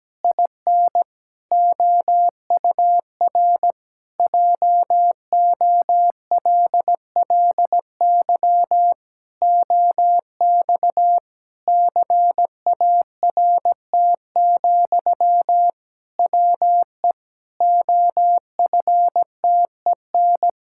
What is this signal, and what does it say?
Morse code, heard as a keyed tone, about 17 wpm, 700 Hz. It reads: IN OUR JOLLY OX CART, WE OFTEN
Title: Pangram Challenge v2 - 17wpm